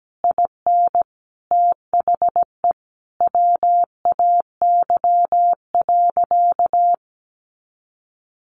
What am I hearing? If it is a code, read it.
Morse code, heard as a keyed tone, about 17 wpm, 700 Hz. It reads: IN THE WAY.
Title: Sentences from Top 400 Words 17wpm